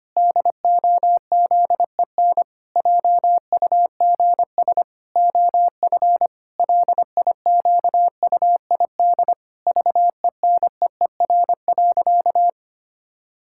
Morse code, heard as a keyed tone, about 25 wpm, 700 Hz. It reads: DOZEN JUGS OF LIQUID VENEER.
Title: Pangram Challenge v2 - 25wpm